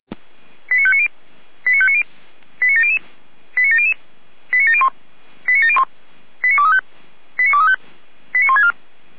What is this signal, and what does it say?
Five-tone selective call (ZVEI), heard as quick runs of five tones, tones of about 70 ms. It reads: 89590 89590 89700 89700 89701 89701 89235 89235 89164
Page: /455/SDR/ZVEI/